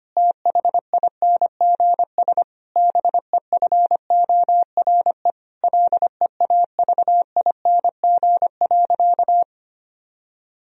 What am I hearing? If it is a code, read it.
Morse code, heard as a keyed tone, about 25 wpm, 700 Hz. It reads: THINGS BEFORE LEAVING.
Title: Sentences from Top 1000 Words - Review Part 4 - 25wpm